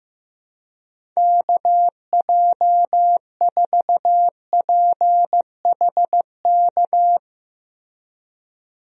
Morse code, heard as a keyed tone, about 15 wpm, 700 Hz. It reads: KJ4PHK
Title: USA Call Signs - 6 Characters - Repeated 3x - 15wpm